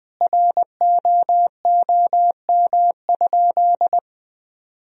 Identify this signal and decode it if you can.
Morse code, heard as a keyed tone, about 19 wpm, 700 Hz. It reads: ROOM?
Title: Sentences from Top 300 Words and Abbreviations 20wpm